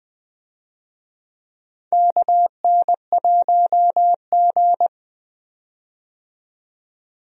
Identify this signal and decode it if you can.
Morse code, heard as a keyed tone, about 21 wpm, 700 Hz. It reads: KN1G
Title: USA Call Signs - 4 Characters - 20wpm